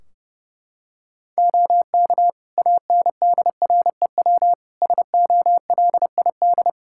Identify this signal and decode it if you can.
Morse code, heard as a keyed tone, about 30 wpm, 700 Hz. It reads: OK ANDREW SOLID
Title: Practice QSO - Exchange 3 - More Info - 30wpm